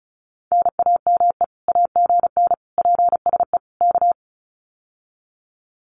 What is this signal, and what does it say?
Morse code, heard as a keyed tone, about 36 wpm, 700 Hz. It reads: NAME AGN PSE K
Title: Common QSO Phrases Encore 35wpm